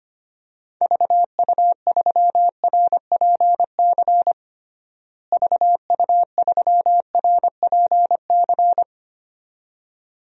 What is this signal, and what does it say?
Morse code, heard as a keyed tone, about 25 wpm, 700 Hz. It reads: VU3RPC VU3RPC
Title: DXCC-100 Call Signs - 6 Characters - Repeated 2x - 25wpm